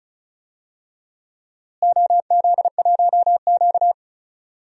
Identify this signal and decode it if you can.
Morse code, heard as a keyed tone, about 36 wpm, 700 Hz. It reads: OZ1Q
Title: DXCC-100 Call Signs - 4 Characters - Repeated 3x - 35wpm